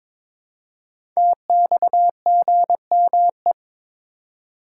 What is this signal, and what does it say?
Morse code, heard as a keyed tone, about 22 wpm, 700 Hz. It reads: TXGME
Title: Code Groups - 5 Letter:Numbers - 22wpm